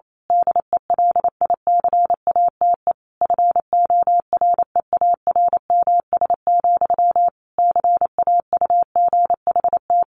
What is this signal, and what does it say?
Morse code, heard as a keyed tone, about 28 wpm, 700 Hz. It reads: DELICATE FOREARMS, CAUGHT
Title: A Sound Of Thunder 28wpm